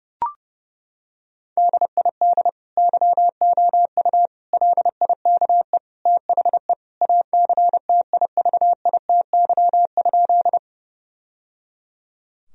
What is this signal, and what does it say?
Morse code, heard as a keyed tone, about 30 wpm, 700 Hz. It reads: DID YOU LIKE THE ACTIVITY?
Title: Sentences from Top 1000 Words - Review Part 4 - 30wpm